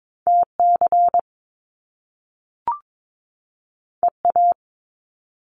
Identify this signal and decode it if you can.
Morse code, heard as a keyed tone, about 22 wpm, 700 Hz. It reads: TC EA
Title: Top Two Letter Combinations in Words 22wpm